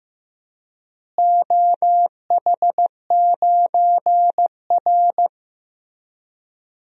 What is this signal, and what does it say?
Morse code, heard as a keyed tone, about 15 wpm, 700 Hz. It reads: OH9R
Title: DXCC-100 Call Signs - 4 Characters - Repeated 3x - 15wpm